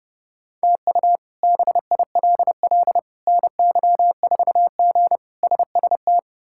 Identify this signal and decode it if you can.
Morse code, heard as a keyed tone, about 30 wpm, 700 Hz. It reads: TU BILL NY4G SST
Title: K1USN SST QSO Practice - 30wpm